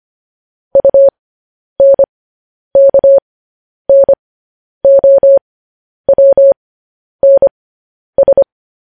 Morse code, heard as a keyed tone, about 25 wpm, spaced as for 10 wpm, 550 Hz. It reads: UNKNOWNS
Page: news headlines in morse code at 10 wpm